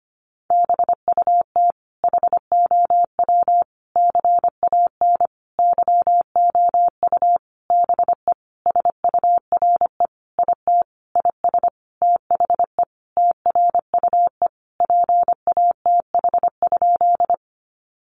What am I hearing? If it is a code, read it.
Morse code, heard as a keyed tone, about 25 wpm, 700 Hz. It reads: BUT HOW CAN YOU BE SURE IT IS THE TRUE PATH?
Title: Sentences from Top 1400 Words 25wpm